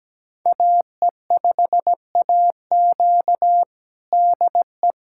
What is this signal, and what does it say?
Morse code, heard as a keyed tone, about 17 wpm, 700 Hz. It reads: AE5AQ DE
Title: USA Callsign DE Callsign - 5 Characters - 17wpm